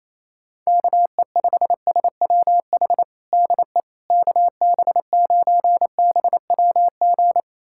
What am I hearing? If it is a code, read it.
Morse code, heard as a keyed tone, about 28 wpm, 700 Hz. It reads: KE5SWH DE KB9BWG
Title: USA Callsign DE Callsign - 6 Characters - 28wpm